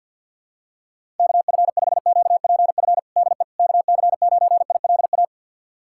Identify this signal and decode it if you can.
Morse code, heard as a keyed tone, about 46 wpm, 700 Hz. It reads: KF5QPF DE KC9ILA
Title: USA Callsign DE Callsign 50wpm